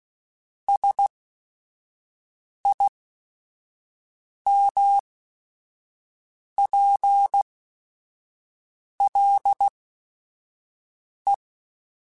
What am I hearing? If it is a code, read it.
Morse code, heard as a keyed tone, about 16 wpm, spaced as for 5 wpm, 800 Hz. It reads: SIMPLE
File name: Technical Subjects / Technical Subjects-mp3s / Sw Pwr Sup @5